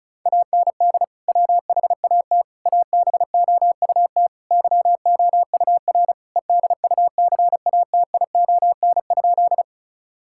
Morse code, heard as a keyed tone, about 34 wpm, 700 Hz. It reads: AND WHAT ABOUT YOUR EDUCATION?
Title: Sentences from Top 1000 Words - Review Part 6 - 35wpm